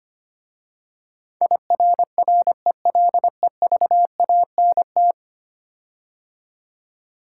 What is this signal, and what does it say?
Morse code, heard as a keyed tone, about 25 wpm, 700 Hz. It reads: IRRELEVANT
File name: 10 Letter Words - First 3 Letters Given - 25wpm